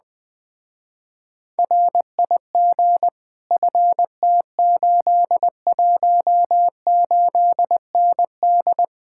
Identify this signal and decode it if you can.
Morse code, heard as a keyed tone, about 20 wpm, 700 Hz. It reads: RIG FT818ND
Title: Speed-Racing Common QSO Phrases - 17wpm